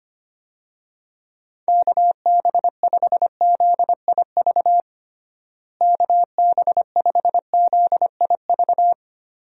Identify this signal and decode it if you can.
Morse code, heard as a keyed tone, about 25 wpm, 700 Hz. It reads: KB5ZIV KB5ZIV
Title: USA Call Signs - 6 Characters - Repeated 3x - 25wpm